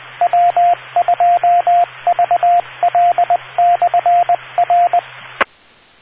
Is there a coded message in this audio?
Yes: Morse code from 0.2 to 5.0 s, heard as a keyed tone, about 20 wpm, 700 Hz, with radio noise. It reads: W2VL/R